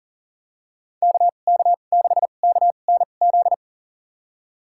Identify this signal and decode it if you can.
Morse code, heard as a keyed tone, about 40 wpm, spaced as for 29 wpm, 700 Hz. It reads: KK6KNZ
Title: USA Call Signs - 6 Characters - 2x Character Spacing - 40wpm